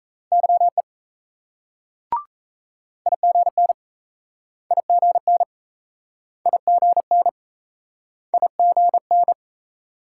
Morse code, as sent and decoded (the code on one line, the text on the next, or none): -.-- . / .. --. -. / .. --. -. / .. --. -. / .. --. -.
YE IGN IGN IGN IGN